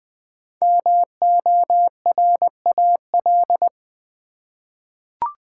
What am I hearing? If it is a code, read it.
Morse code, heard as a keyed tone, about 20 wpm, 700 Hz. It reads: MORAL
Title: 5 Letter Words - First 3 Letters Given - 20wpm